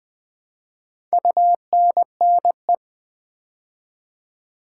Morse code, heard as a keyed tone, about 20 wpm, 700 Hz. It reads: UNNE
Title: Build A Word - 4 Letters At A Time - 11-Letter Words - 20wpm